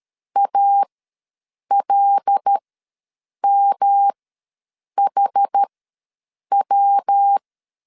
Morse code, heard as a keyed tone, about 13 wpm, spaced as for 7 wpm, 800 Hz. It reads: ALMHW
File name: LEZIONI / 5 LBVC / 5 L ws 12wpm 2958